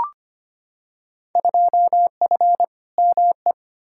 Morse code, heard as a keyed tone, about 25 wpm, 700 Hz. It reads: 2F ME
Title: Field Day Exchanges - 25wpm